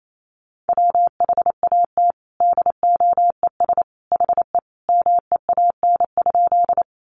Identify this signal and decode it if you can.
Morse code, heard as a keyed tone, about 28 wpm, 700 Hz. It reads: WHAT DOES HE MEAN?
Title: Sentences from Top 200 Words 28wpm